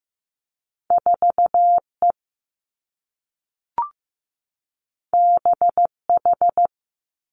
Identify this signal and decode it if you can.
Morse code, heard as a keyed tone, about 15 wpm, 700 Hz. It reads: 4 E B H
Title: Two Challenging Letter-Number Combinations 15wpm